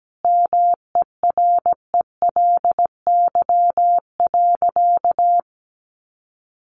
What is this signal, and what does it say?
Morse code, heard as a keyed tone, about 17 wpm, 700 Hz. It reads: MERELY.
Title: Sentences from Top 1400 Words 17wpm